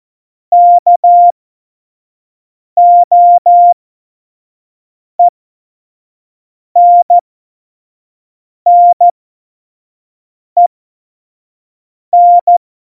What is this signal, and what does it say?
Morse code, heard as a keyed tone, about 14 wpm, spaced as for 5 wpm, 700 Hz. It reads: KOENNEN
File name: Morsekurs / Witze / Witz 07. November 2017, 25 F70